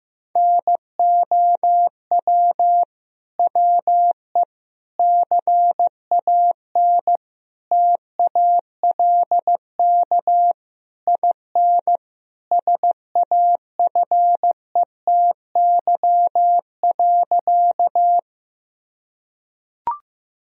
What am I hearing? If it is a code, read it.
Morse code, heard as a keyed tone, about 16 wpm, 700 Hz. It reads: NOW WE CAN TALK IN SAFETY.
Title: Sentences from Top 1000 Words - Review Part 4 - 15wpm